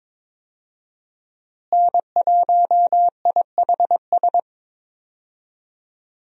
Morse code, heard as a keyed tone, about 22 wpm, 700 Hz. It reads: N1IHS
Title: USA Call Signs - 5 Characters - Repeated 3x - 22wpm